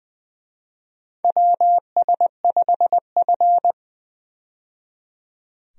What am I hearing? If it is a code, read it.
Morse code, heard as a keyed tone, about 20 wpm, 700 Hz. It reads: WS5F
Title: USA Call Signs - 4 Characters - 20wpm